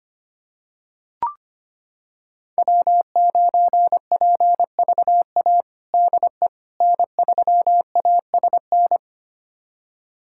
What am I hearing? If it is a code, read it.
Morse code, heard as a keyed tone, about 25 wpm, 700 Hz. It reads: W9PVA DE N3ASN
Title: USA Callsign DE Callsign - 5 Characters - 25wpm